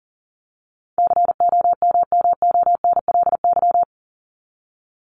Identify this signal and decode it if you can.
Morse code, heard as a keyed tone, about 40 wpm, 700 Hz. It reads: COMMONLY
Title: LY Suffix - Top 100 Words with Base Words 40wpm